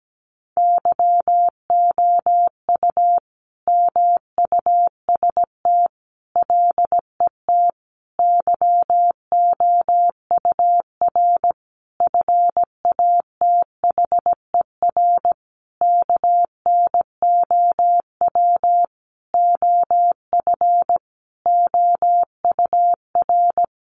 Morse code, heard as a keyed tone, about 17 wpm, 700 Hz. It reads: YOU MUST LET YOUR FATHER KNOW OF OUR
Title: Sentences from Top 1500 Words 17wpm